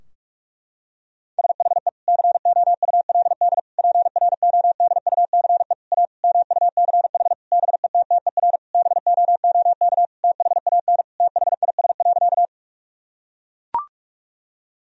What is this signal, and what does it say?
Morse code, heard as a keyed tone, about 41 wpm, 700 Hz. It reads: SHE COULD PRODUCE A MUCH BETTER BOOK THAN THIS.
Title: Sentences from Top 1000 Words - Review Part 3 - 45wpm